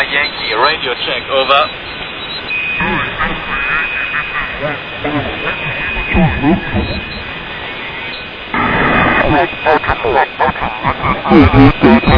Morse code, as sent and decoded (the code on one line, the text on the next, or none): none